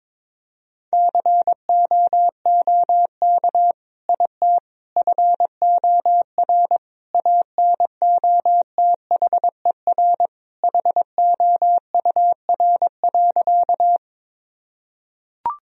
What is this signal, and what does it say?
Morse code, heard as a keyed tone, about 22 wpm, 700 Hz. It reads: COOK IT FOR ANOTHER HOUR.
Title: Sentences from Top 1000 Words - Review Part 4 - 22wpm